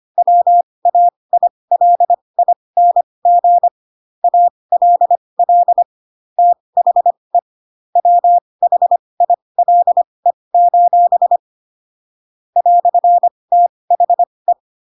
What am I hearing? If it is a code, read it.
Morse code, heard as a keyed tone, about 25 wpm, spaced as for 20 wpm, 700 Hz. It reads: WAILING ALL THE WHILE: "THE